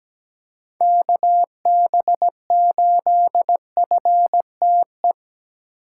Morse code, heard as a keyed tone, about 17 wpm, 700 Hz. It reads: KB8FTE
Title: USA Call Signs - 6 Characters - 2x Character Spacing - 17wpm